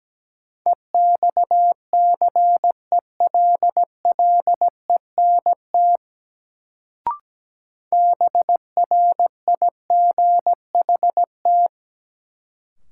Morse code, heard as a keyed tone, about 17 wpm, 700 Hz. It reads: EXCELLENT BRIGHT
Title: Sentences from Top 1000 Words - New Word Drill - 17wpm